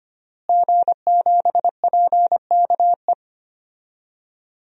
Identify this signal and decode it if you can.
Morse code, heard as a keyed tone, about 25 wpm, 700 Hz. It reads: G7PKE
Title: DXCC-100 Call Signs - 5 Characters - Repeated 3x - 25wpm